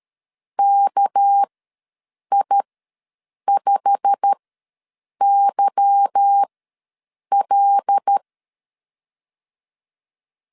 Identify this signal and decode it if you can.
Morse code, heard as a keyed tone, about 13 wpm, spaced as for 7 wpm, 800 Hz. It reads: KI5YL
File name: LEZIONI / 8 YJ45 / 8 YJ45 ws 12wpm 3146